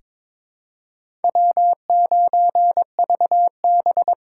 Morse code, heard as a keyed tone, about 21 wpm, 700 Hz. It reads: W9VB
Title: USA Call Signs - 4 Characters - 4x Character Spacing - 22wpm